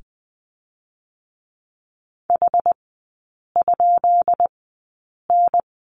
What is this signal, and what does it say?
Morse code, heard as a keyed tone, about 20 wpm, spaced as for 8 wpm, 690 Hz. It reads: H?N
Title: 133 Lesson 22e - Question Mark - Three Characters 20wpm